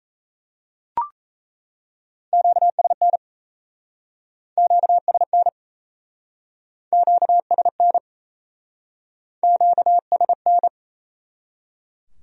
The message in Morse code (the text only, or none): QSN QSN QSN QSN